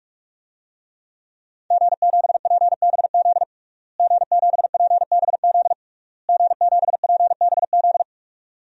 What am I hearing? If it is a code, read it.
Morse code, heard as a keyed tone, about 46 wpm, 700 Hz. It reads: G7PBZ G7PBZ G7PBZ
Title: DXCC-100 Call Signs - 5 Characters - Repeated 3x - 45wpm